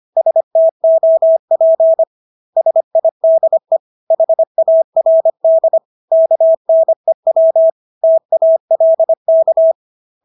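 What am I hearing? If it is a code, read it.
Morse code, heard as a keyed tone, about 25 wpm, 650 Hz. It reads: STOP SIDE HARD KNEW TALK
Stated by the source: Morse practice of English words at 25 wpm from CW med Gnister.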